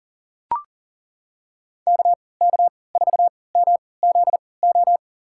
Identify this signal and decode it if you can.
Morse code, heard as a keyed tone, about 40 wpm, spaced as for 23 wpm, 700 Hz. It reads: KK4MZO
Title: USA Call Signs - 3x Character Spacing - 40wpm